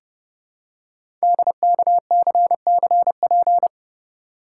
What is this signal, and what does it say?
Morse code, heard as a keyed tone, about 29 wpm, 700 Hz. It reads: DKCCP
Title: Code Groups - 5 Letters - 30wpm